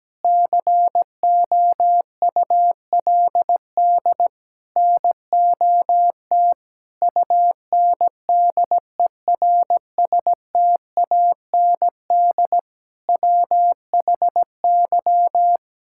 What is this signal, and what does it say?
Morse code, heard as a keyed tone, about 17 wpm, 700 Hz. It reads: COULD NOT UNDERSTAND WHY
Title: Sentences from Top 1000 Words - Review Part 6 - 17wpm